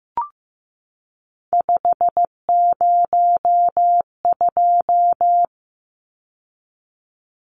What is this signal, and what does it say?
Morse code, heard as a keyed tone, about 15 wpm, 700 Hz. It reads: 502
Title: Numbers - 3-Digit 15wpm